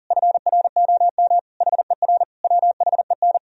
Morse code, heard as a keyed tone, about 39 wpm, 700 Hz. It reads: FROM HER WHEN
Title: Sentences from Top 1000 Words - Review Part 3 - 40wpm